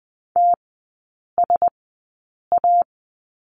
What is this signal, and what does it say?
Morse code, heard as a keyed tone, about 20 wpm, spaced as for 8 wpm, 700 Hz. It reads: TSA
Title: 035 Lesson 7e - Letter s - Three Characters 20wpm